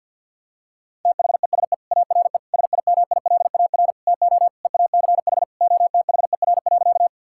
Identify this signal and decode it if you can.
Morse code, heard as a keyed tone, about 51 wpm, 700 Hz. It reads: THESE ARE SIMILAR TO EACH OTHER.
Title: Sentences from Top 1000 Words - Review Part 5 - 50wpm